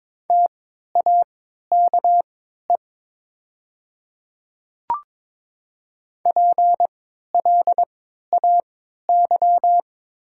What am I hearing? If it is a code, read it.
Morse code, heard as a keyed tone, about 22 wpm, spaced as for 13 wpm, 700 Hz. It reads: TAKE PLAY
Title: 3x Character Spacing - Top 200 Words - 22wpm